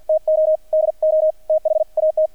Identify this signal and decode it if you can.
Morse code, heard as a keyed tone, about 48 wpm, spaced as for 42 wpm, 630 Hz, with radio noise. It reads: TO DO THAT
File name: archived / CRSnet